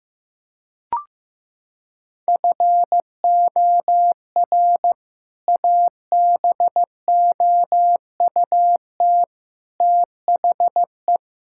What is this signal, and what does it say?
Morse code, heard as a keyed tone, about 15 wpm, 700 Hz. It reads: FOR ABOUT THE
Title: Sentences from Top 1000 Words - Review Part 5 - 15wpm